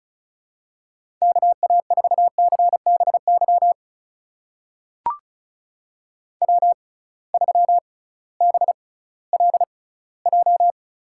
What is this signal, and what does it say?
Morse code, heard as a keyed tone, about 35 wpm, spaced as for 19 wpm, 700 Hz. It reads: KA4CBY W3BLJ
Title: USA Call Signs - 6x Character Spacing - 35wpm